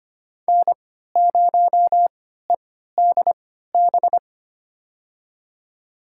Morse code, heard as a keyed tone, about 25 wpm, spaced as for 14 wpm, 700 Hz. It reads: N0EDB
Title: USA Call Signs - 5 Characters - 3x Character Spacing - 25wpm